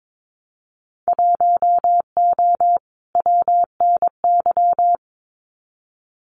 Morse code, heard as a keyed tone, about 22 wpm, 700 Hz. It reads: 1O WNY
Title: Winter Field Day Exchanges 22wpm